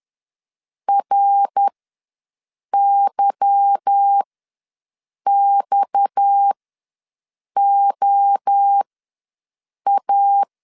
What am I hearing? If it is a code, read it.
Morse code, heard as a keyed tone, about 10 wpm, spaced as for 6 wpm, 800 Hz. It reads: RYXOA